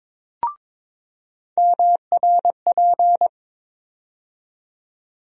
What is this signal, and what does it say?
Morse code, heard as a keyed tone, about 22 wpm, 700 Hz. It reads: MRP
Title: USA Call Sign Postfixes - 3 Letters 22wpm